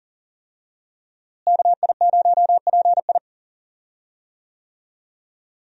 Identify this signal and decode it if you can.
Morse code, heard as a keyed tone, about 40 wpm, 700 Hz. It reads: KI0PI
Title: USA Call Signs - 5 Characters - Repeated 2x - 40wpm